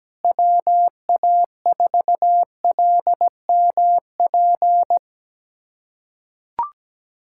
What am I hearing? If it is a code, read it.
Morse code, heard as a keyed tone, about 17 wpm, 700 Hz. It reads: WA4LMP
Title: USA Callsign DE Callsign - 6 Characters - 17wpm